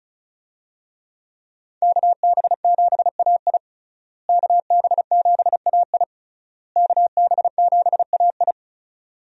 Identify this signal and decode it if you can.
Morse code, heard as a keyed tone, about 36 wpm, 700 Hz. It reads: KB7AI KB7AI KB7AI
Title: USA Call Signs - 5 Characters - Repeated 3x - 35wpm